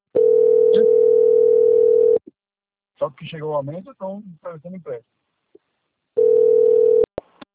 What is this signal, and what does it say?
Ringback tone (North America), 440 + 480 Hz, with speech.
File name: out- - - - - .638